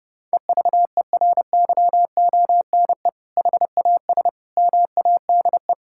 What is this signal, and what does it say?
Morse code, heard as a keyed tone, about 30 wpm, 700 Hz. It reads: EVERYONE HAS MADE
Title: Sentences from Top 1000 Words - Review Part 5 - 30wpm